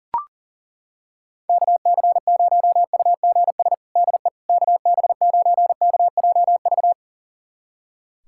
Morse code, heard as a keyed tone, about 40 wpm, 700 Hz. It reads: KC0UGS DE KB9KJV
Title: USA Callsign DE Callsign - 6 Characters - 40wpm